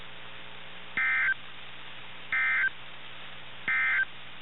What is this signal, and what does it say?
Emergency Alert System (SAME) tones, heard as harsh buzzing bursts. It is heard from 0.9 to 4.1 s.
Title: NOAA Weather Radio 162.4